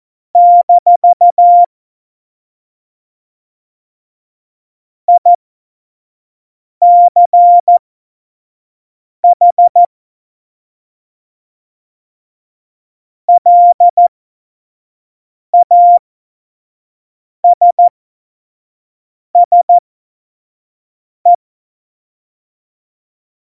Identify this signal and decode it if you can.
Morse code, heard as a keyed tone, about 14 wpm, spaced as for 5 wpm, 700 Hz. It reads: - ICH LASSE
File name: Morsekurs / Witze / Witz 06. November 2018, 25 F70